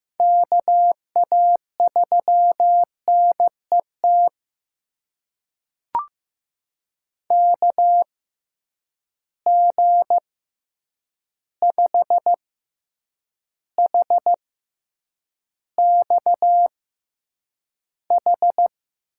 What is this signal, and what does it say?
Morse code, heard as a keyed tone, about 16 wpm, spaced as for 8 wpm, 700 Hz. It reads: KA3NET KG5HXH
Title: USA Call Signs - 6 Characters - 6x Character Spacing - 15wpm